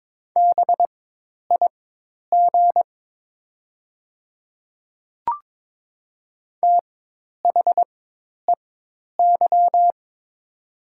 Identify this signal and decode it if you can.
Morse code, heard as a keyed tone, about 22 wpm, spaced as for 10 wpm, 700 Hz. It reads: BIG THEY
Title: 4x Character Spacing - Top 200 Words - 22wpm